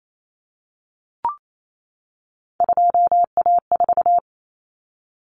Morse code, heard as a keyed tone, about 28 wpm, 700 Hz. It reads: 2A4
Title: Common Cut Numbers - 3-Digit 28wpm